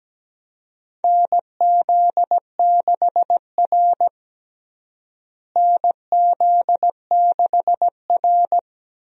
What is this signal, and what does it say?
Morse code, heard as a keyed tone, about 17 wpm, 700 Hz. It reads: NZ6R NZ6R
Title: USA Call Signs - 4 Characters - Repeated 2x - 17wpm